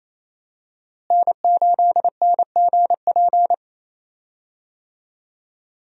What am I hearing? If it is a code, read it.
Morse code, heard as a keyed tone, about 28 wpm, 700 Hz. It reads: N8NGP
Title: USA Call Signs - 5 Characters - Repeated 2x - 28wpm